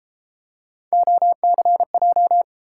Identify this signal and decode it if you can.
Morse code, heard as a keyed tone, about 32 wpm, 700 Hz. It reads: OCJ